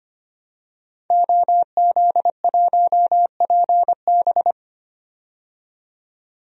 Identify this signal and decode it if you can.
Morse code, heard as a keyed tone, about 25 wpm, 700 Hz. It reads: OZ1PB
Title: DXCC-100 Call Signs - 5 Characters - 25wpm